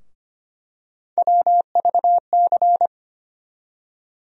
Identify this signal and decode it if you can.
Morse code, heard as a keyed tone, about 25 wpm, 700 Hz. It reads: WVC
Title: USA Call Sign Postfixes - 3 Letters 25wpm